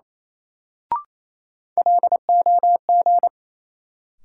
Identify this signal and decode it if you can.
Morse code, heard as a keyed tone, about 28 wpm, 700 Hz. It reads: LOG
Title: Sentences from Top 1400 Words - New Word Drill - 28wpm